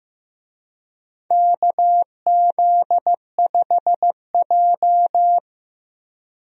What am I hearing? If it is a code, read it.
Morse code, heard as a keyed tone, about 15 wpm, 700 Hz. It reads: KZ5J
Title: USA Call Signs - 4 Characters - Repeated 2x - 15wpm